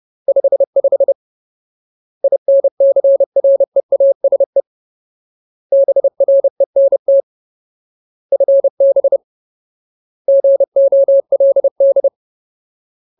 Morse code, heard as a keyed tone, about 30 wpm, 550 Hz. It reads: 55 INCREASE BRENT FB GOLD
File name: Ultimate Word List / 30WPM Ultimate